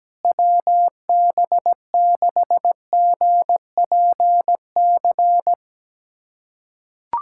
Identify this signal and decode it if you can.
Morse code, heard as a keyed tone, about 17 wpm, 700 Hz. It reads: WB6GPC
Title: USA Callsign DE Callsign - 6 Characters - 17wpm